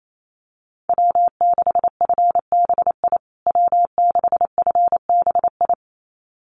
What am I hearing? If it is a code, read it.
Morse code, heard as a keyed tone, about 28 wpm, 700 Hz. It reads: W6FBI W6FBI
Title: USA Call Signs Repeated 2x 28wpm